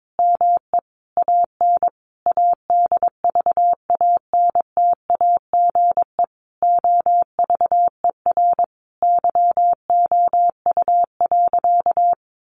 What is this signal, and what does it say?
Morse code, heard as a keyed tone, about 22 wpm, 700 Hz. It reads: ME AN ADVANTAGE OVER YOU.
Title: Sentences from Top 900 Words 22wpm